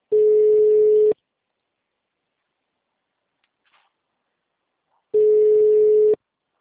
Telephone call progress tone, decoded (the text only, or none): ringback tone